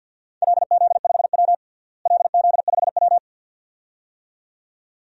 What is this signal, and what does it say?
Morse code, heard as a keyed tone, about 50 wpm, 700 Hz. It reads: LZ5W LZ5W
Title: DXCC-100 Call Signs - 4 Characters - Repeated 2x - 50wpm